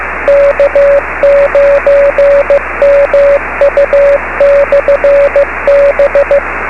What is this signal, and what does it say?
Morse code, heard as a keyed tone, about 15 wpm, 560 Hz, with radio noise. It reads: K9MU/B
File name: beacons / k9mu